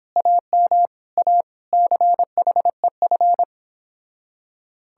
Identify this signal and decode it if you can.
Morse code, heard as a keyed tone, about 29 wpm, 700 Hz. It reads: AM A CHEF